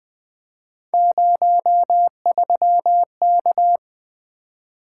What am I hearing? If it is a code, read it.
Morse code, heard as a keyed tone, about 20 wpm, 700 Hz. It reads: 03K
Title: Three Letter-Number Combinations - 20wpm